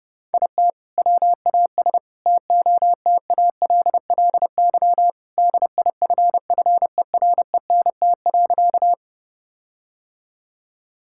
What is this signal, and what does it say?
Morse code, heard as a keyed tone, about 31 wpm, 700 Hz. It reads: IT WAS TOTALLY DIFFERENT.
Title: Sentences from Top 1600 Words 30wpm